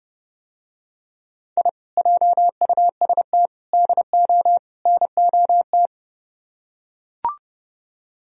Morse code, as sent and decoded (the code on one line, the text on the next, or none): .. / .--- ..- ... - / -.. --- / -. --- -
I JUST DO NOT